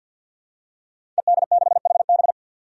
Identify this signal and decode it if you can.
Morse code, heard as a keyed tone, about 51 wpm, 700 Hz. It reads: ED6HB
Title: Five Challenging Letter-Number Combinations 50wpm